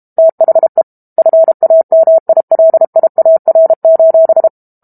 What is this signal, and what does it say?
Morse code, heard as a keyed tone, about 32 wpm, 650 Hz. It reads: THE FAMILIAR:
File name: Book Two / 06-Chapter 6 - 32 WPM